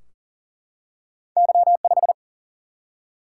Morse code, heard as a keyed tone, about 40 wpm, 700 Hz. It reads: Y 5
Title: Single-Letter Single-Number 40wpm